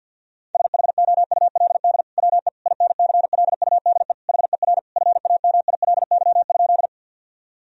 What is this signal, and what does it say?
Morse code, heard as a keyed tone, about 50 wpm, 700 Hz. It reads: SHOULD WE INCLUDE HER FAMILY?
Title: Sentences from Top 1000 Words - Review Part 3 - 50wpm